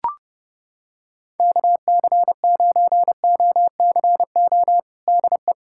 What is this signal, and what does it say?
Morse code, heard as a keyed tone, about 30 wpm, 700 Hz. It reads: KC9OCO DE
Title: USA Callsign DE Callsign 30wpm